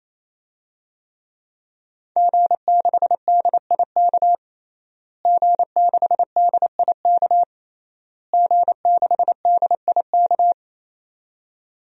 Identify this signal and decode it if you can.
Morse code, heard as a keyed tone, about 27 wpm, 700 Hz. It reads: G6DIK G6DIK G6DIK
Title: DXCC-100 Call Signs - 5 Characters - Repeated 3x - 28wpm